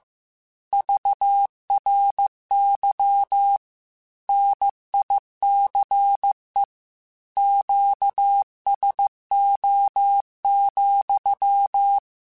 Morse code, heard as a keyed tone, about 15 wpm, 800 Hz. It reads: VRY NICE QSO,